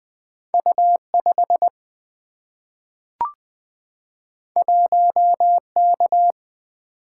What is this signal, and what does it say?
Morse code, heard as a keyed tone, about 20 wpm, 690 Hz. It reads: U5 1K
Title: Two Letter-Number Combinations - 20wpm